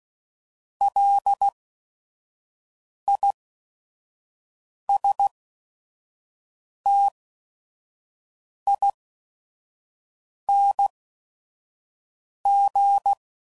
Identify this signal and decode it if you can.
Morse code, heard as a keyed tone, about 16 wpm, spaced as for 5 wpm, 800 Hz. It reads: LISTING